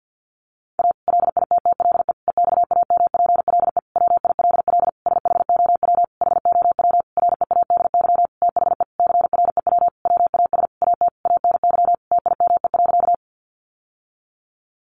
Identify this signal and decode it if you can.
Morse code, heard as a keyed tone, about 49 wpm, 700 Hz. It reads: A LITTLE EXAMPLE WILL SHOW HOW READY THE CREW WAS AT ANY TIME.
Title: Sentences from Top 1500 Words 50wpm